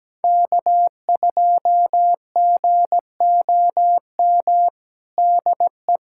Morse code, heard as a keyed tone, about 17 wpm, 700 Hz. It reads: K2GOM DE
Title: USA Callsign DE Callsign - 5 Characters - 17wpm